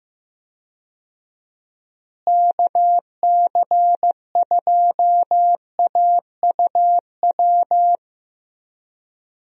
Morse code, heard as a keyed tone, about 15 wpm, 700 Hz. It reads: KC2AUW
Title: USA Call Signs - 6 Characters - Repeated 2x - 15wpm